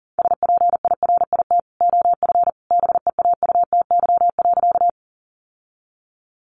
Morse code, heard as a keyed tone, about 40 wpm, 700 Hz. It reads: SPIRIT OF BEAUTY.
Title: Sentences from Top 1400 Words 40wpm